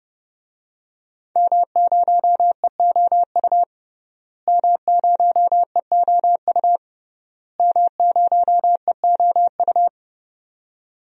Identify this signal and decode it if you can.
Morse code, heard as a keyed tone, about 30 wpm, 700 Hz. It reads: M0EOU M0EOU M0EOU
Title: DXCC-100 Call Signs - 5 Characters - Repeated 3x - 30wpm